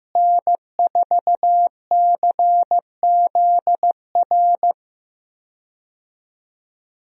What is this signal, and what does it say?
Morse code, heard as a keyed tone, about 15 wpm, 700 Hz. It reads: N4CZR
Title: USA Call Signs - 5 Characters - Repeated 2x - 15wpm